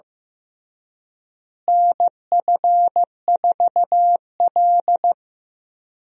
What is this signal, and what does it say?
Morse code, heard as a keyed tone, about 15 wpm, 700 Hz. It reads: NF4L
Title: USA Call Signs - 4 Characters - Repeated 3x - 15wpm